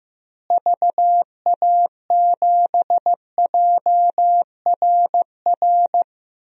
Morse code, heard as a keyed tone, about 16 wpm, 700 Hz. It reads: VA7JRR
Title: DXCC-100 Call Signs - 6 Characters - Repeated 3x - 15wpm